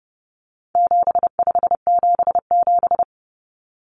Morse code, heard as a keyed tone, about 30 wpm, 700 Hz. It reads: 7577
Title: Common Cut Numbers - 4-Digit 30wpm